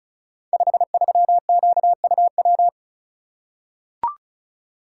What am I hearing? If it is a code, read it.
Morse code, heard as a keyed tone, about 36 wpm, 700 Hz. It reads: 53QUW
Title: Code Groups - 5 Letter:Numbers - 35wpm